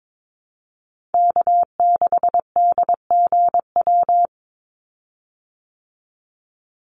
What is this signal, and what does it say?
Morse code, heard as a keyed tone, about 22 wpm, 700 Hz. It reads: K6DGW
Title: Call Signs Repeated 3x - 22wpm